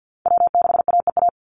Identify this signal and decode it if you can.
Morse code, heard as a keyed tone, about 50 wpm, 700 Hz. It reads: W6AEA
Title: Call Signs - 50wpm